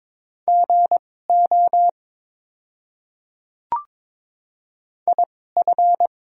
Morse code, heard as a keyed tone, about 22 wpm, spaced as for 16 wpm, 700 Hz. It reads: GO IF
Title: 2x Character Spacing - Top 100 Words - 22wpm